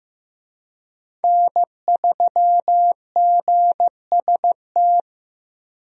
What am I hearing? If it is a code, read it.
Morse code, heard as a keyed tone, about 15 wpm, 700 Hz. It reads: N3GST
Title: USA Call Signs - 5 Characters - 15wpm